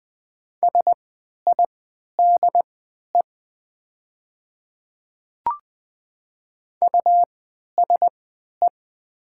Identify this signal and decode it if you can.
Morse code, heard as a keyed tone, about 21 wpm, spaced as for 12 wpm, 700 Hz. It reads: SIDE USE
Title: 3x Character Spacing - Top 100 Words - 20wpm